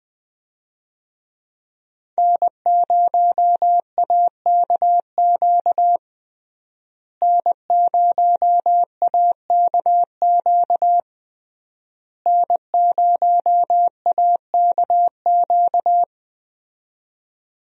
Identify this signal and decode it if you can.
Morse code, heard as a keyed tone, about 21 wpm, 700 Hz. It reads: N0AKQ N0AKQ N0AKQ
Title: USA Call Signs - 5 Characters - Repeated 3x - 20wpm